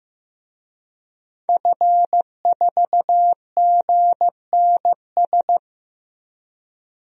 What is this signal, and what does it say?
Morse code, heard as a keyed tone, about 15 wpm, 700 Hz. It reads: F4GNS
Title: DXCC-100 Call Signs - 5 Characters - 15wpm